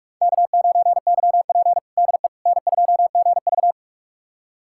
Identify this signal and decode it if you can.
Morse code, heard as a keyed tone, about 41 wpm, 700 Hz. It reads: K9YP DE N2GV
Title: USA Callsign DE Callsign - 4 Characters - 45wpm